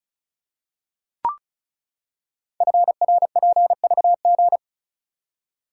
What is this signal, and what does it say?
Morse code, heard as a keyed tone, about 36 wpm, 690 Hz. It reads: FRPVG
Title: Code Groups - 5 Letters - 35wpm